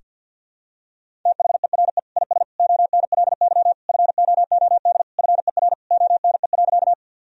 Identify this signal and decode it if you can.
Morse code, heard as a keyed tone, about 50 wpm, 700 Hz. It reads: THERE IS ONLY FOOD FER ONE.
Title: Sentences from Top 300 Words and Abbreviations 50wpm